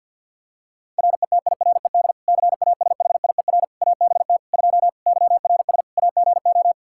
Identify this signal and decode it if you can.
Morse code, heard as a keyed tone, about 46 wpm, 700 Hz. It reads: RETIRED CASHIER ABT 2 YRS AGO
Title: K1USN SST QSO Practice 50wpm